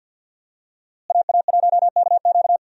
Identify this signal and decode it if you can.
Morse code, heard as a keyed tone, about 52 wpm, 700 Hz. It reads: AA1KQ
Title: USA Call Signs - 5 Characters - Repeated 3x - 50wpm